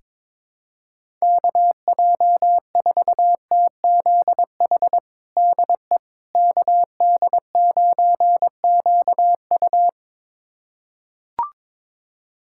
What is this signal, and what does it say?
Morse code, heard as a keyed tone, about 22 wpm, 700 Hz. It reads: KJ4TZH DE KD9QU
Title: USA Callsign DE Callsign 22wpm